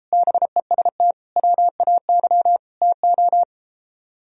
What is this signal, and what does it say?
Morse code, heard as a keyed tone, about 33 wpm, 700 Hz. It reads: BEST WAY TO